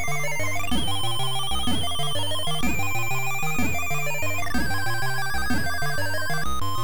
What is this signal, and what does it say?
There is no code.